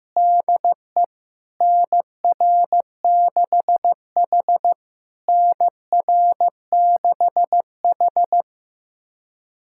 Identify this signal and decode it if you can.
Morse code, heard as a keyed tone, about 15 wpm, 700 Hz. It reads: DE NR6H NR6H
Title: SOTA QSO Practice 15wpm